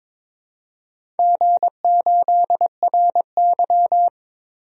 Morse code, heard as a keyed tone, about 22 wpm, 700 Hz. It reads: G8RY
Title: DXCC-100 Call Signs - 4 Characters - Repeated 3x - 22wpm